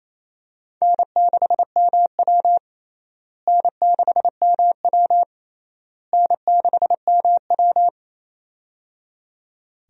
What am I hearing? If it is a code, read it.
Morse code, heard as a keyed tone, about 27 wpm, 700 Hz. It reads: N6MW N6MW N6MW
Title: USA Call Signs - 4 Characters - Repeated 3x - 28wpm